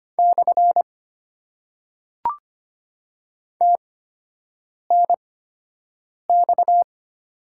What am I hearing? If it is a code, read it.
Morse code, heard as a keyed tone, about 25 wpm, spaced as for 7 wpm, 700 Hz. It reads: / TNX
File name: Top 90 QSO Elements - 8x Character Spacing - 25wpm